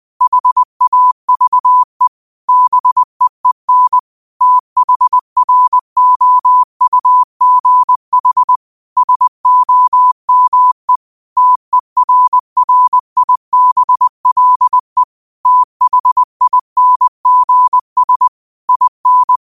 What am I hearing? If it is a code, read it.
Morse code, heard as a keyed tone, about 20 wpm, 1000 Hz. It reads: HAVE BEEN THROUGH SOME TERRIBLE THINGS IN